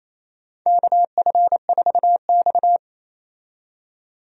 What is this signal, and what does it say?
Morse code, heard as a keyed tone, about 28 wpm, 700 Hz. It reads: KF4X
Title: USA Call Signs - 4 Characters - 28wpm